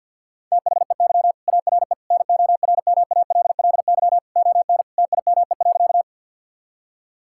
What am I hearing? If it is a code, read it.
Morse code, heard as a keyed tone, about 50 wpm, 700 Hz. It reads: THEY ARE NORMALLY ON TIME.
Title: Sentences from Top 1000 Words - Review Part 6 - 50wpm